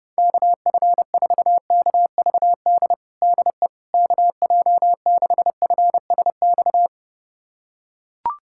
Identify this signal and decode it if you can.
Morse code, heard as a keyed tone, about 30 wpm, 700 Hz. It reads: KF4KVD DE KJ6FSX
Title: USA Callsign DE Callsign - 6 Characters - 30wpm